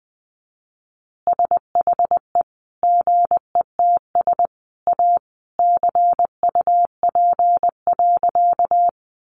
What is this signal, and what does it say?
Morse code, heard as a keyed tone, about 20 wpm, 700 Hz. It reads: SHE GETS A CUP.
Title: Easy Phrases 20wpm